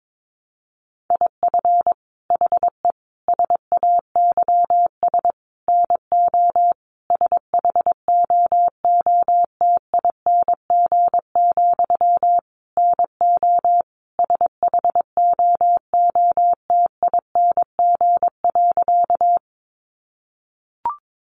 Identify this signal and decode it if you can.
Morse code, heard as a keyed tone, about 22 wpm, 700 Hz. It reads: IF HE SAYS NO SHOOTING, NO SHOOTING.
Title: A Sound Of Thunder 22wpm